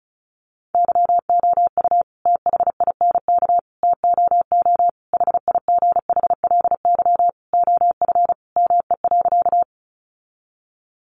Morse code, heard as a keyed tone, about 34 wpm, 700 Hz. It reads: YOU THINK TOO HIGHLY OF ME.
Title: Sentences from Top 1100 Words 35wpm